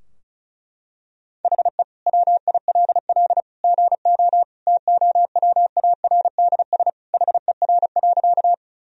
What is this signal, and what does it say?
Morse code, heard as a keyed tone, about 36 wpm, 700 Hz. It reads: HE WILL GO TOWARDS HER.
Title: Sentences from Top 1000 Words - Review Part 2 - 35wpm